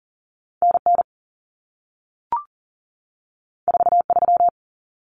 Morse code, heard as a keyed tone, about 40 wpm, 700 Hz. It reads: NN 43
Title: Common Cut Numbers - 2-Digit 40wpm